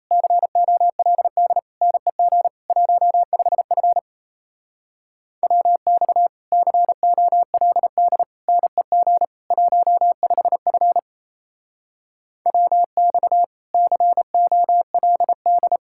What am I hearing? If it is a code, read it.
Morse code, heard as a keyed tone, about 35 wpm, 700 Hz. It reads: COLD NEG 15F WX COLD NEG 15F WX COLD
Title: Speed-Racing Common QSO Phrases - 28wpm